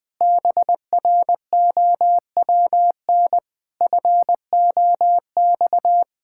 Morse code, heard as a keyed tone, about 20 wpm, 700 Hz. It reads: BROWN FOX
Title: Sentences with 8-Word Phrases - 20wpm